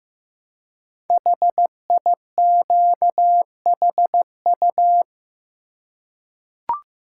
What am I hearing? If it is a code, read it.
Morse code, heard as a keyed tone, about 15 wpm, 700 Hz. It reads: HIQHU
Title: Code Groups - 5 Letters - 15wpm